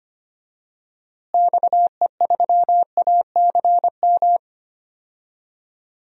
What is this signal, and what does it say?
Morse code, heard as a keyed tone, about 25 wpm, 700 Hz. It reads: XE3ACM
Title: DXCC-100 Call Signs - 6 Characters - Repeated 3x - 25wpm